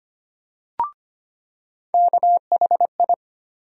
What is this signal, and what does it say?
Morse code, heard as a keyed tone, about 25 wpm, 700 Hz. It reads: KHI
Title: USA Call Sign Postfixes - 3 Letters 25wpm